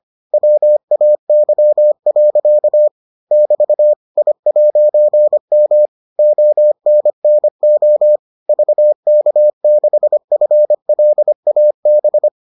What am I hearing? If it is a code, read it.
Morse code, heard as a keyed tone, about 25 wpm, 600 Hz. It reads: WAY. <BT> I'M ONNO VK6FLAB